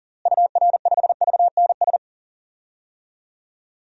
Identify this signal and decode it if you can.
Morse code, heard as a keyed tone, about 40 wpm, 700 Hz. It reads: UR5VNS
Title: DXCC-100 Call Signs - 6 Characters - Repeated 2x - 40wpm